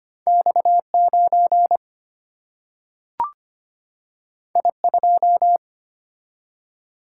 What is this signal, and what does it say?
Morse code, heard as a keyed tone, about 25 wpm, 700 Hz. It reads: X 9 I 2
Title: Single-Letter Single-Number 25wpm